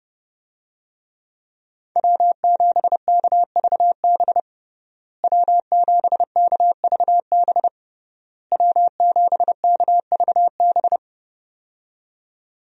Morse code, heard as a keyed tone, about 29 wpm, 700 Hz. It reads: W7KVB W7KVB W7KVB
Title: USA Call Signs - 5 Characters - Repeated 3x - 30wpm